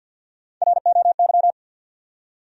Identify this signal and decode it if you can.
Morse code, heard as a keyed tone, about 50 wpm, 700 Hz. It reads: ROY
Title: USA Common Names - 50wpm